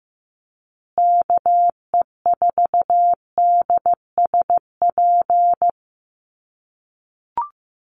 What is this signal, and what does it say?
Morse code, heard as a keyed tone, about 15 wpm, 700 Hz. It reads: KE4DSP
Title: USA Call Signs Repeated 3x 15wpm